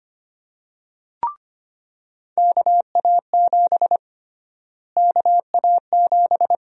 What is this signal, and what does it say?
Morse code, heard as a keyed tone, about 25 wpm, 700 Hz. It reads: KA7 KA7
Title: USA Call Sign Prefixes - Repeated 2x - 25wpm